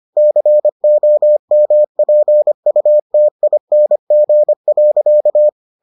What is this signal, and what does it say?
Morse code, heard as a keyed tone, about 25 wpm, 600 Hz. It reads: COMPUTING.